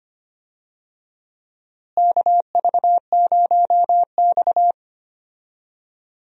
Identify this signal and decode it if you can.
Morse code, heard as a keyed tone, about 25 wpm, 700 Hz. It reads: KV0X
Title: USA Call Signs - 4 Characters - Repeated 2x - 25wpm